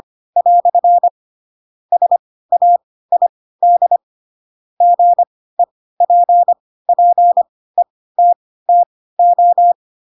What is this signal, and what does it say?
Morse code, heard as a keyed tone, about 25 wpm, spaced as for 16 wpm, 700 Hz. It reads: " SAID GEPPETTO
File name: Pinocchio / Pinocchio ch3 10 25 ik1hge